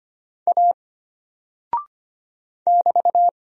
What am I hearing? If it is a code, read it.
Morse code, heard as a keyed tone, about 25 wpm, 700 Hz. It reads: A <BT>
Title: Single Character (Letter:Number:Punctuation:Prosign) Alt 25wpm